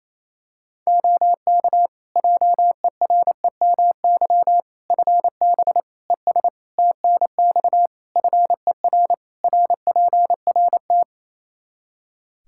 Morse code, heard as a keyed tone, about 28 wpm, 700 Hz. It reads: OK JEREMY FB ES TNX FER RPRT
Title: Practice QSO - Exchange 2 - Info - 28wpm